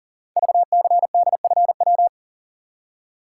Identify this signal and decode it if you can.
Morse code, heard as a keyed tone, about 38 wpm, 700 Hz. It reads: VCDFW
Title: Code Groups - 5 Letters - 40wpm